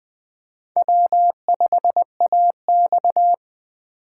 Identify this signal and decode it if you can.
Morse code, heard as a keyed tone, about 20 wpm, 700 Hz. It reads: W5AX
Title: Speed Racing - USA 1x2 Call Signs 17wpm